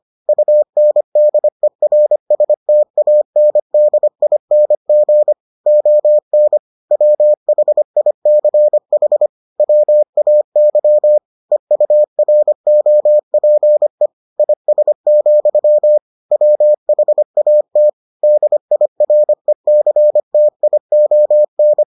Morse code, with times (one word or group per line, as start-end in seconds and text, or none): 0.3-5.3 UNDERSTANDING
5.7-6.6 ON
6.9-9.3 WHICH
9.6-11.2 WAY
11.5-14.1 EUROPE
14.4-16.0 IS,
16.3-17.9 WHAT
18.2-21.8 DIRECTION